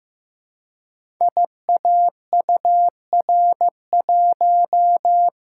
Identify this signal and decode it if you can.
Morse code, heard as a keyed tone, about 15 wpm, 700 Hz. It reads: IAUR1
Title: Code Groups - 5 Letter:Numbers - 15wpm